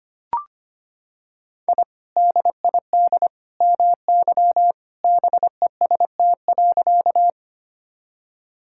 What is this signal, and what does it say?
Morse code, heard as a keyed tone, about 25 wpm, 700 Hz. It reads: I DID MY BEST.
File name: practice / 25wpm / Sentences with 2-Word Phrases - 25wpm